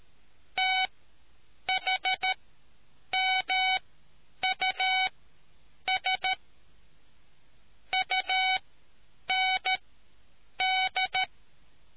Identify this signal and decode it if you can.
Morse code, heard as a keyed tone, about 13 wpm, spaced as for 8 wpm, 750 Hz. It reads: THMUS UND